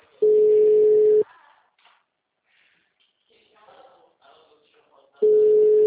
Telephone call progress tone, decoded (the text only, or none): ringback tone